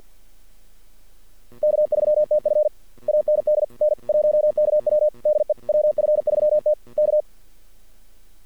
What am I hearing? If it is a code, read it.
Morse code, heard as a keyed tone, about 48 wpm, 600 Hz. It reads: K3TF AND N1KW DE WF3T K
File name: archived / CRSnet